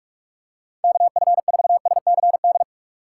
Morse code, heard as a keyed tone, about 45 wpm, 700 Hz. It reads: KF4SCD
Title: USA Call Signs - 6 Characters - 2x Character Spacing - 45wpm